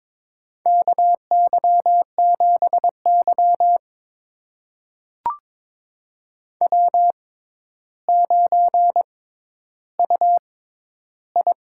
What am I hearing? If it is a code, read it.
Morse code, heard as a keyed tone, about 22 wpm, spaced as for 11 wpm, 700 Hz. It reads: KY7Y W9UI
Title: USA Call Signs - 4 Characters - 6x Character Spacing - 22wpm